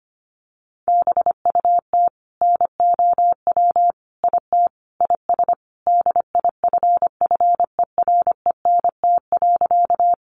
Morse code, heard as a keyed tone, about 25 wpm, 700 Hz. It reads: BUT NOW IT IS DIFFERENT.
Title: Sentences from Top 600 Words 25wpm